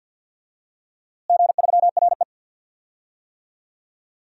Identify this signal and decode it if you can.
Morse code, heard as a keyed tone, about 49 wpm, 700 Hz. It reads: G3RE
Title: DXCC-100 Call Signs - 4 Characters - Repeated 3x - 50wpm